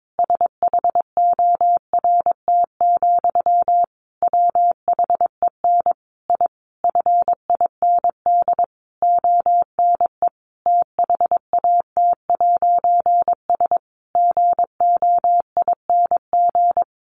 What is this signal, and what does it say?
Morse code, heard as a keyed tone, about 22 wpm, 700 Hz. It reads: SHORT, WHEN I FIND ONE THAT'S GOING
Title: A Sound Of Thunder 22wpm